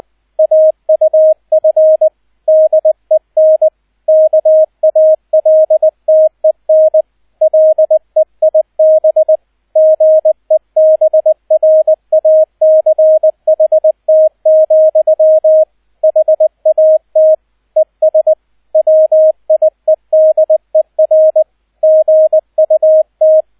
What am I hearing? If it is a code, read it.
Morse code, heard as a keyed tone, about 20 wpm, 630 Hz, with radio noise. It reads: AUF DEN KALTEN LEIB GEBRACHT, HAT ES WIEDER GUT